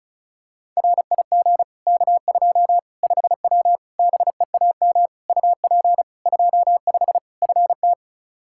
Morse code, heard as a keyed tone, about 36 wpm, 700 Hz. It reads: RIG K2 5W BEAM UP 25 FT
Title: Practice QSO - Exchange 2 - Info - 35wpm